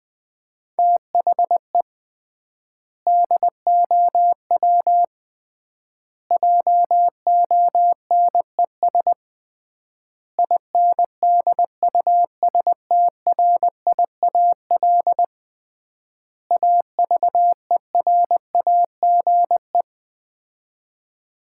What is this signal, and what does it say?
Morse code, heard as a keyed tone, about 20 wpm, 700 Hz. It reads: THE DOW JONES INDUSTRIAL AVERAGE
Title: 3x Word-Spacing - Sets of 5 Words - 20wpm